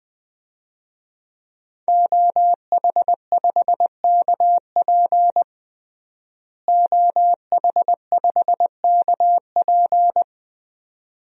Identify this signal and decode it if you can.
Morse code, heard as a keyed tone, about 20 wpm, 700 Hz. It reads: OH5KP OH5KP
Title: DXCC-100 Call Signs - 5 Characters - Repeated 2x - 20wpm